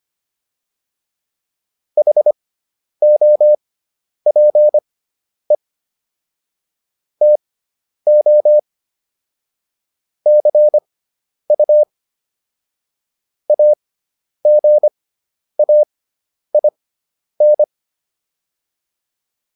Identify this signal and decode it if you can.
Morse code, heard as a keyed tone, about 25 wpm, spaced as for 10 wpm, 600 Hz. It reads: HOPE TO CU AGAIN